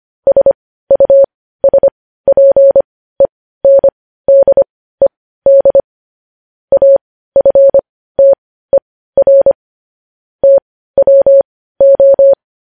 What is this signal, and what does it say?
Morse code, heard as a keyed tone, about 25 wpm, spaced as for 15 wpm, 550 Hz. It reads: SUSPENDED AFTER TWO
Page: News Headlines in Morse Code at 15 WPM